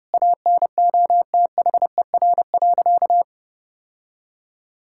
Morse code, heard as a keyed tone, about 32 wpm, 700 Hz. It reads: ANOTHER.
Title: Sentences from Top 1000 Words - Review Part 6 - 30wpm